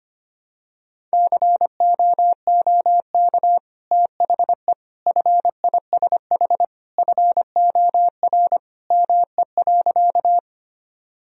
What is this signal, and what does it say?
Morse code, heard as a keyed tone, about 25 wpm, 700 Hz. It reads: COOK THE FISH FOR ME.
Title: Sentences from Top 1000 Words - Review Part 4 - 25wpm